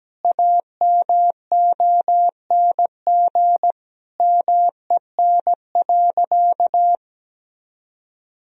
Morse code, heard as a keyed tone, about 17 wpm, 700 Hz. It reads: AMONG MEN.
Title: Sentences from Top 1000 Words - Review Part 3 - 17wpm